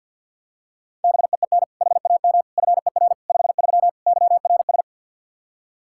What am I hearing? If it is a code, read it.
Morse code, heard as a keyed tone, about 51 wpm, 700 Hz. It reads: BEEN HAM FER 53 YRS
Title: Practice QSO - Exchange 3 - More Info - 50wpm